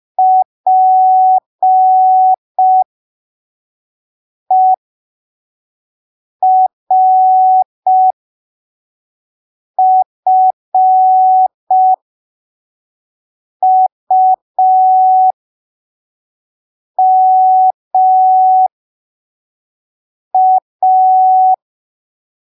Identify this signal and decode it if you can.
Morse code, heard as a keyed tone, about 5 wpm, spaced as for 3 wpm, 750 Hz. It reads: PERFUMA
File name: arquivos / texto / texto 20 05ppm